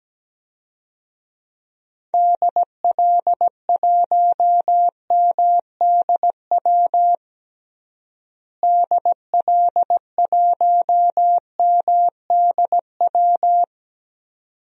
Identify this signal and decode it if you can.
Morse code, heard as a keyed tone, about 17 wpm, 700 Hz. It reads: DL1MDW DL1MDW
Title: DXCC-100 Call Signs - 6 Characters - Repeated 2x - 17wpm